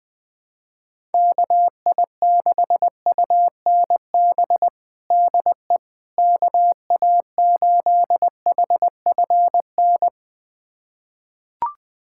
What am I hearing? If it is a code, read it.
Morse code, heard as a keyed tone, about 20 wpm, 700 Hz. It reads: KI6UNB DE KA8HFN
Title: USA Callsign DE Callsign - 6 Characters - 20wpm